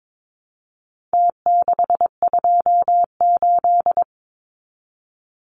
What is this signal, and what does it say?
Morse code, heard as a keyed tone, about 22 wpm, 700 Hz. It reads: T628
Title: Numbers - Contest Style - 4-Digit 22wpm